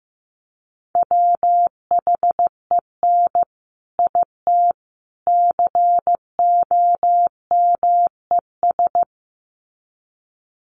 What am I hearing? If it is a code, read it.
Morse code, heard as a keyed tone, about 15 wpm, 700 Hz. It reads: WHEN IT COMES
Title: Sets of 3 Words 15wpm